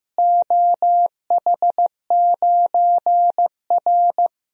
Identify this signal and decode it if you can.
Morse code, heard as a keyed tone, about 15 wpm, 700 Hz. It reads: OH9R
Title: DXCC-100 Call Signs - 4 Characters - Repeated 3x - 15wpm